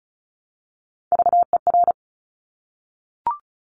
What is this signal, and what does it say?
Morse code, heard as a keyed tone, about 36 wpm, 700 Hz. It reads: VER
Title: Top Three Letter Combinations in Words 35wpm